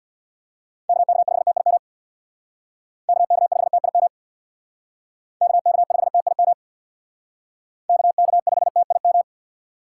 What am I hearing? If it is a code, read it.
Morse code, heard as a keyed tone, about 58 wpm, 700 Hz. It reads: KK5TIM KK5TIM KK5TIM KK5TIM
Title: Speed Racing - USA 2x3 Call Signs 50wpm